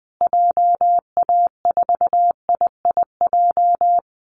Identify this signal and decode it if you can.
Morse code, heard as a keyed tone, about 20 wpm, 700 Hz. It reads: JA4IIJ
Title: Call Signs Repeated 2x - 20wpm